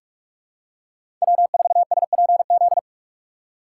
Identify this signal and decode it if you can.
Morse code, heard as a keyed tone, about 42 wpm, 700 Hz. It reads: W4SPZ
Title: USA Call Signs - 5 Characters - 2x Character Spacing - 45wpm